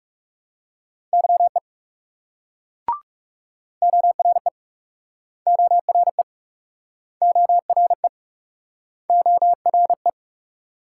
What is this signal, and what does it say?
Morse code, heard as a keyed tone, about 38 wpm, 700 Hz. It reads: YE ORE ORE ORE ORE